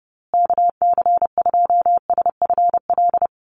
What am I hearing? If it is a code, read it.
Morse code, heard as a keyed tone, about 30 wpm, 700 Hz. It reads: KC2SFL
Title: USA Call Signs Repeated 2x 30wpm